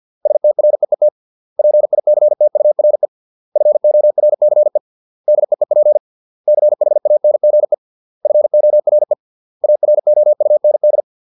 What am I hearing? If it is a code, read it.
Morse code, heard as a keyed tone, about 50 wpm, 600 Hz. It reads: STREET PICTURE FORCE DEEP CHANGE FORE AROUND